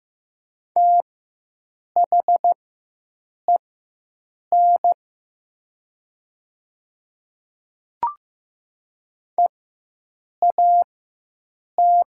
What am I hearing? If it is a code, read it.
Morse code, heard as a keyed tone, about 15 wpm, spaced as for 7 wpm, 700 Hz. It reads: THEN EAT
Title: 4x Character Spacing - Top 500 Words - 15wpm